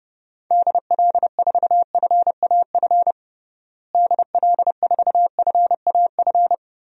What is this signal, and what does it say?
Morse code, heard as a keyed tone, about 30 wpm, 700 Hz. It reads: DL4FAF DL4FAF
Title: DXCC-100 Call Signs - 6 Characters - Repeated 3x - 30wpm